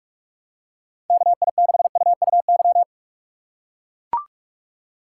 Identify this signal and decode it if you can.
Morse code, heard as a keyed tone, about 42 wpm, 700 Hz. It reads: KI6UUY
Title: USA Call Signs - 6 Characters - 3x Character Spacing - 45wpm